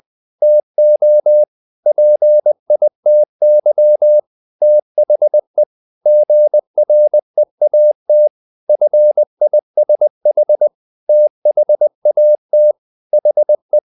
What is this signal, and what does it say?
Morse code, heard as a keyed tone, about 20 wpm, 600 Hz. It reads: TO PITY THE GREAT FISH THAT HE
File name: the old man and the sea / the old man and the sea part 1 20wpm